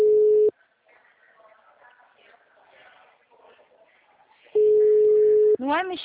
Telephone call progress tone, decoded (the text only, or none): ringback tone